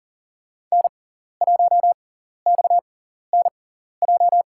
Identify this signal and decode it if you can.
Morse code, heard as a keyed tone, about 40 wpm, spaced as for 14 wpm, 700 Hz. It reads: N1XNJ
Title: USA Call Signs - 5 Characters - 6x Character Spacing - 40wpm